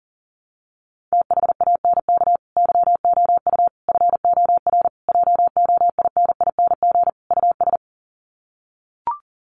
Morse code, heard as a keyed tone, about 40 wpm, 700 Hz. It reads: THANK YOU FOR JOINING US
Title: Sets of 5 Words 40wpm